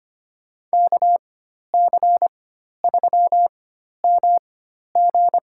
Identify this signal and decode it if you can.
Morse code, heard as a keyed tone, about 25 wpm, spaced as for 12 wpm, 700 Hz. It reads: KC3MG
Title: USA Call Signs - 4x Character Spacing - 25wpm